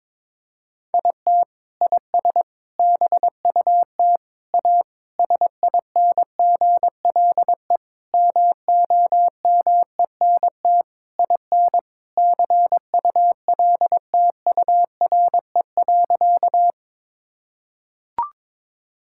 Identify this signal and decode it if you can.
Morse code, heard as a keyed tone, about 22 wpm, 700 Hz. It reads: IT IS BUT A SINGLE MOMENT IN CULTURE.
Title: Sentences from Top 1000 Words - Review Part 6 - 22wpm